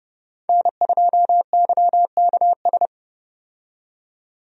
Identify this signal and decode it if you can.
Morse code, heard as a keyed tone, about 32 wpm, 700 Hz. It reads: N2YKS
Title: USA Call Signs - 5 Characters - 30wpm